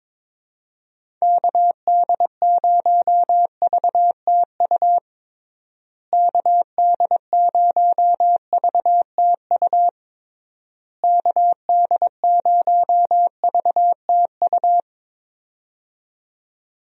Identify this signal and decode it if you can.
Morse code, heard as a keyed tone, about 22 wpm, 700 Hz. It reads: KD0VTU KD0VTU KD0VTU
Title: USA Call Signs - 6 Characters - Repeated 3x - 22wpm